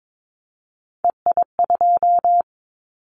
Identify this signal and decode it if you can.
Morse code, heard as a keyed tone, about 22 wpm, 700 Hz. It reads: EI2
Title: Callsign Prefixes for Top 50 DXCC - 22wpm